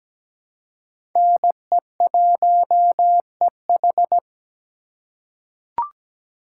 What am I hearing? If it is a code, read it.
Morse code, heard as a keyed tone, about 17 wpm, 700 Hz. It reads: NE1EH
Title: USA Call Signs - 5 Characters - 3x Character Spacing - 17wpm